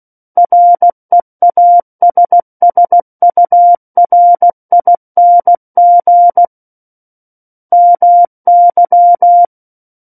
Morse code, heard as a keyed tone, about 16 wpm, 700 Hz. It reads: REASSURING MY
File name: Book One / 07-Chapter 7 - 16 WPM